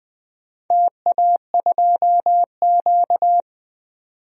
Morse code, heard as a keyed tone, about 20 wpm, 700 Hz. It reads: TA2Q
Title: DXCC-100 Call Signs - 4 Characters - 20wpm